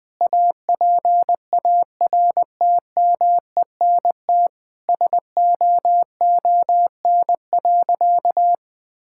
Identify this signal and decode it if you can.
Morse code, heard as a keyed tone, about 20 wpm, 700 Hz. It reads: APARTMENT SOON.
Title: Sentences from Top 1000 Words - Review Part 6 - 20wpm